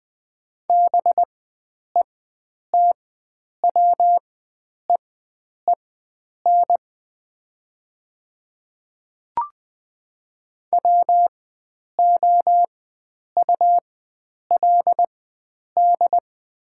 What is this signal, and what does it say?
Morse code, heard as a keyed tone, about 20 wpm, spaced as for 9 wpm, 700 Hz. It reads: BETWEEN WOULD
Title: 4x Character Spacing - Top 300 Words - 20wpm